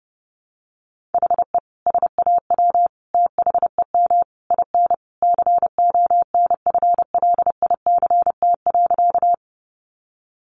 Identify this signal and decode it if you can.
Morse code, heard as a keyed tone, about 30 wpm, 700 Hz. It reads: HE SAW THEM IN CONFLICT.
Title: Sentences from Top 1300 Words 30wpm